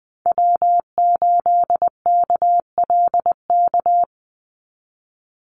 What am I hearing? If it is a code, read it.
Morse code, heard as a keyed tone, about 20 wpm, 700 Hz. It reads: W8KLK
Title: USA Call Signs 20wpm